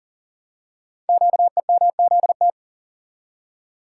Morse code, heard as a keyed tone, about 39 wpm, 690 Hz. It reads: QEMZT
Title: Code Groups - 5 Letters - 40wpm